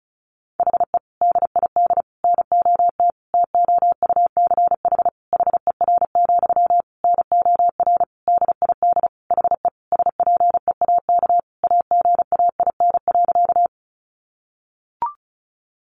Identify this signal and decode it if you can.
Morse code, heard as a keyed tone, about 34 wpm, 700 Hz. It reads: HE DID NOT TOUCH HER, NOR DID HE SPEAK AGAIN.
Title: Sentences from Top 1500 Words 35wpm